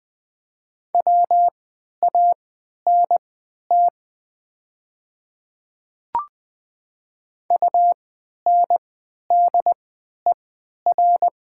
Morse code, heard as a keyed tone, about 20 wpm, spaced as for 11 wpm, 700 Hz. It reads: WANT UNDER
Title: 3x Character Spacing - Top 200 Words - 20wpm